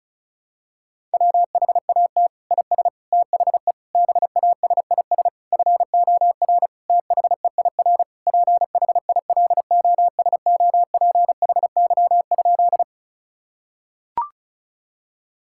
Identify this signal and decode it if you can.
Morse code, heard as a keyed tone, about 36 wpm, 700 Hz. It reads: WHAT IS THE BASIS FOR THEIR PHILOSOPHY?
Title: Sentences from Top 1000 Words - Review Part 6 - 35wpm